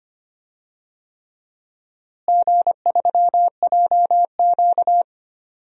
Morse code, heard as a keyed tone, about 25 wpm, 700 Hz. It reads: G3JQ
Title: DXCC-100 Call Signs - 4 Characters - Repeated 3x - 25wpm